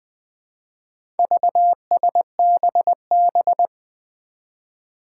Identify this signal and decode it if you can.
Morse code, heard as a keyed tone, about 21 wpm, 700 Hz. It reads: VSBB
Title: Four Challenging Letter-Number Combinations 20wpm